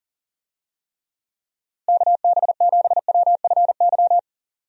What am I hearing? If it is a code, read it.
Morse code, heard as a keyed tone, about 40 wpm, 700 Hz. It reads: KB7WFY
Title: USA Call Signs - 6 Characters - Repeated 2x - 40wpm